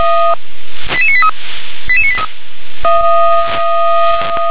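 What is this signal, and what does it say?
Five-tone selective call (ZVEI), heard as quick runs of five tones, tones of about 70 ms. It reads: 88983 88983
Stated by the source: ZVEI five-tone call received by SDR.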